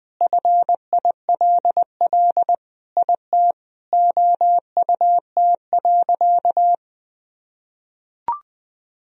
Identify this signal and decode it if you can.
Morse code, heard as a keyed tone, about 20 wpm, 700 Hz. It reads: FILL IT OUT.
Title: Sentences from Top 1000 Words - Review Part 3 - 20wpm